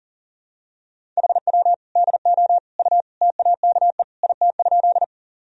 Morse code, heard as a keyed tone, about 40 wpm, 700 Hz. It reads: HW DO U TAKE IT?
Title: Sentences from Top 200 Words and Abbreviations 40wpm